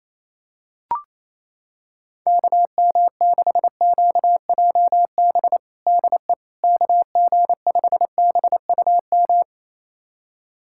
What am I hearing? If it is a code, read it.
Morse code, heard as a keyed tone, about 27 wpm, 700 Hz. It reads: KM6QJB DE KG5BUM
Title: USA Callsign DE Callsign - 6 Characters - 28wpm